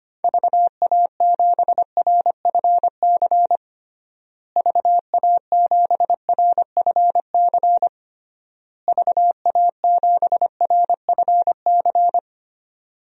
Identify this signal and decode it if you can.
Morse code, heard as a keyed tone, about 25 wpm, 700 Hz. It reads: VA7RFC VA7RFC VA7RFC
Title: DXCC-100 Call Signs - 6 Characters - Repeated 3x - 25wpm